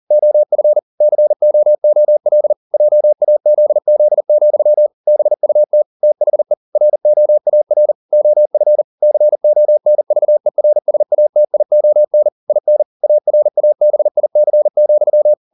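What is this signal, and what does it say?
Morse code, heard as a keyed tone, about 40 wpm, 600 Hz. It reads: OF COOL JAZZ, BUT THE ROAR OF CONVERSATION IN ARABIC,
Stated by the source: Morse code at 40 wpm, practice text, CW med Gnister.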